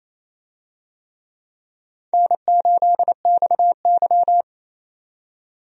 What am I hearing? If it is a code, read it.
Morse code, heard as a keyed tone, about 28 wpm, 700 Hz. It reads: N8XY
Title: USA Call Signs - 4 Characters - 28wpm